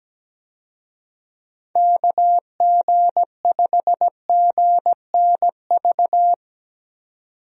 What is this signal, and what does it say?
Morse code, heard as a keyed tone, about 17 wpm, 700 Hz. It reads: KG5GNV
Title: USA Call Signs - 6 Characters - 17wpm